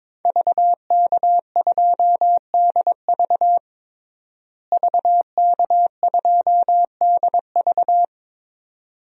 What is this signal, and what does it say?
Morse code, heard as a keyed tone, about 21 wpm, 700 Hz. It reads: VK2DV VK2DV
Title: DXCC-100 Call Signs - 5 Characters - Repeated 2x - 22wpm